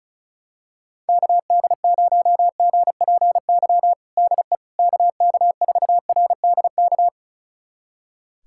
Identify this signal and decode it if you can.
Morse code, heard as a keyed tone, about 34 wpm, 700 Hz. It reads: KD0GPY DE KK4RDK
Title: USA Callsign DE Callsign 35wpm